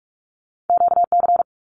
Morse code, heard as a keyed tone, about 46 wpm, 700 Hz. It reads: QC
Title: Canadian Provinces and Territories - Repeated 3x - 45wpm